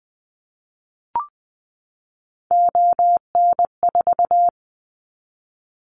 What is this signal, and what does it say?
Morse code, heard as a keyed tone, about 20 wpm, 700 Hz. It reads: ON4
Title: Callsign Prefixes for Top 50 DXCC - 20wpm